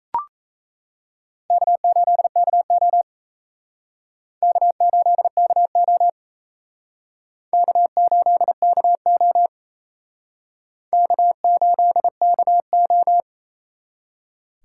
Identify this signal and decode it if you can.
Morse code, heard as a keyed tone, about 35 wpm, 700 Hz. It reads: K8KO K8KO K8KO K8KO